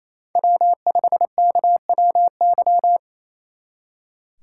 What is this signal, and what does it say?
Morse code, heard as a keyed tone, about 34 wpm, 700 Hz. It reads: W5KWY
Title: Speed Racing - USA Call Signs 28wpm